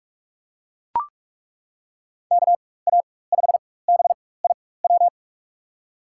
Morse code, heard as a keyed tone, about 42 wpm, spaced as for 20 wpm, 700 Hz. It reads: KA5BIW
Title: USA Call Signs - 6 Characters - 4x Character Spacing - 45wpm